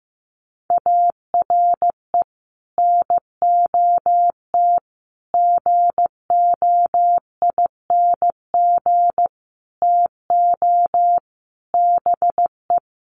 Morse code, heard as a keyed tone, about 15 wpm, 700 Hz. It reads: ARE NOT GOING TO BE
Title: Sets of 5 Words Encore 15wpm